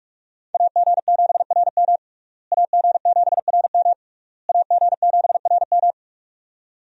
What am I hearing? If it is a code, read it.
Morse code, heard as a keyed tone, about 46 wpm, 700 Hz. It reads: AG7RM AG7RM AG7RM
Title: USA Call Signs - 5 Characters - Repeated 3x - 45wpm